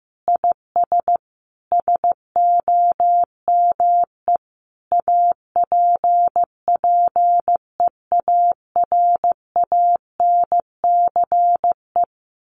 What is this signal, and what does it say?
Morse code, heard as a keyed tone, about 14 wpm, 700 Hz. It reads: IS SOME APPEARANCE
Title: Sentences from Top 1100 Words 15wpm